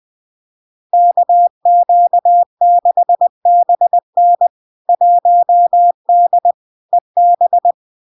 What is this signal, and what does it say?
Morse code, heard as a keyed tone, about 20 wpm, 700 Hz. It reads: KQ6BN 1D EB
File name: LARK-FD-2022-Log-in-CW-20wpm-700-Hz